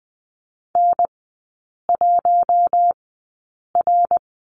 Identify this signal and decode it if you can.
Morse code, heard as a keyed tone, about 20 wpm, spaced as for 8 wpm, 700 Hz. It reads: N1R
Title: 053 Lesson 10e - Letter r - Three Characters 20wpm